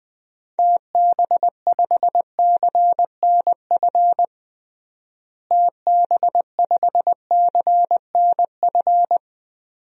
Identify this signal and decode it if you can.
Morse code, heard as a keyed tone, about 20 wpm, 700 Hz. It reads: TB5CNF TB5CNF
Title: DXCC-100 Call Signs - 6 Characters - Repeated 2x - 20wpm